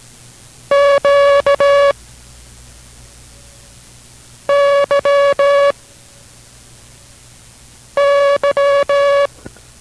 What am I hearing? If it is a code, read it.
Morse code, heard as a keyed tone, about 18 wpm, 560 Hz. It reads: Q Y Y